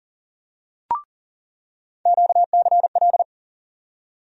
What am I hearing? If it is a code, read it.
Morse code, heard as a keyed tone, about 40 wpm, 700 Hz. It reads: QCL
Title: USA Call Sign Postfixes - 3 Letters 40wpm